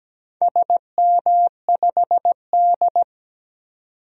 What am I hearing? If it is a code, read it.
Morse code, heard as a keyed tone, about 17 wpm, 700 Hz. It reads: SM5D
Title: DXCC-100 Call Signs - 4 Characters - 17wpm